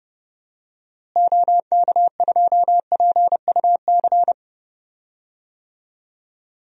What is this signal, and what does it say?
Morse code, heard as a keyed tone, about 32 wpm, 700 Hz. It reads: OK2PUC
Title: DXCC-100 Call Signs - 6 Characters - Repeated 3x - 30wpm